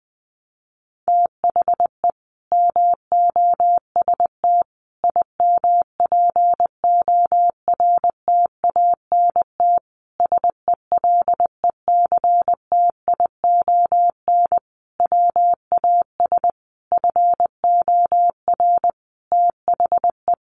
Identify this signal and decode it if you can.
Morse code, heard as a keyed tone, about 20 wpm, 700 Hz. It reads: THE MOST IMPORTANT SELECTION WAS FOR THE
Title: Sentences from Top 1400 Words 20wpm